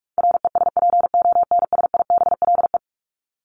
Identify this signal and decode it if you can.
Morse code, heard as a keyed tone, about 44 wpm, 700 Hz. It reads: RESPONSIBLE
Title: Top 751-1000 Words 45wpm